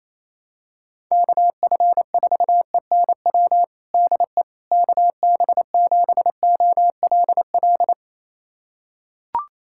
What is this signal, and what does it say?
Morse code, heard as a keyed tone, about 28 wpm, 700 Hz. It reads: KF4ENW DE KB7OLL
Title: USA Callsign DE Callsign - 6 Characters - 28wpm